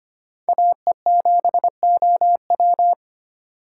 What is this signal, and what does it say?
Morse code, heard as a keyed tone, about 25 wpm, 700 Hz. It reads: AE7OW
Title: USA Call Signs - 5 Characters - 4x Character Spacing - 25wpm